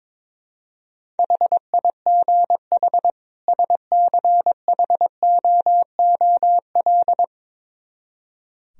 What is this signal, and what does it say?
Morse code, heard as a keyed tone, about 23 wpm, 700 Hz. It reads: HIGH SCHOOL
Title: Speed-Racing - Sets of 2 Words 22wpm